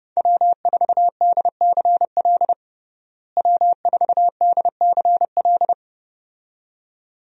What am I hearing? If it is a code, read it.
Morse code, heard as a keyed tone, about 32 wpm, 700 Hz. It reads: W4DCL W4DCL
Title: USA Call Signs - 5 Characters - Repeated 2x - 30wpm